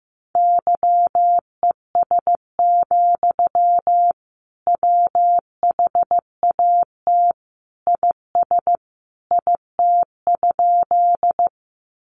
Morse code, heard as a keyed tone, about 15 wpm, 700 Hz. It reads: YES, WHAT IS IT?
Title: Sentences from Top 500 Words 15wpm